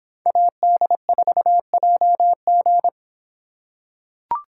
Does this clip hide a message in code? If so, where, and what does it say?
Yes: Morse code from 0.3 to 2.9 s, heard as a keyed tone, about 26 wpm, 700 Hz. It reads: AD4JG